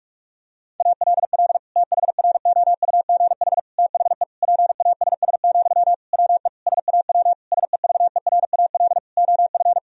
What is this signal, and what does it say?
Morse code, heard as a keyed tone, about 46 wpm, 700 Hz. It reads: ALL THROUGH THE PASS, WE SAW SEVERAL OF
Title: Sentences from Top 1000 Words - Review Part 2 - 45wpm